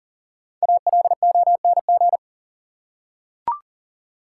Morse code, heard as a keyed tone, about 40 wpm, 700 Hz. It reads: ALONG
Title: Five Letter Words - 40wpm